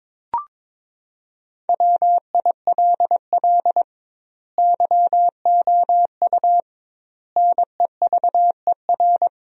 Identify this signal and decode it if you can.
Morse code, heard as a keyed tone, about 21 wpm, 700 Hz. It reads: WILL YOU NEVER
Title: 2x Word-Spacing - Sentences from Top 300 Words - 22wpm